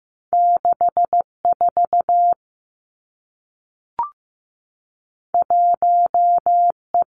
Morse code, heard as a keyed tone, about 15 wpm, 700 Hz. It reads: 6 4 1 E
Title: Two Challenging Letter-Number Combinations 15wpm